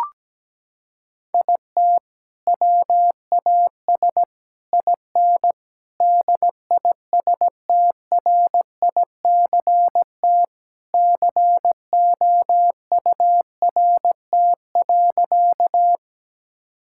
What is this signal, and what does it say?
Morse code, heard as a keyed tone, about 17 wpm, 700 Hz. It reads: IT WAS IN DISTRICT COURT.
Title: Sentences from Top 1600 Words 17wpm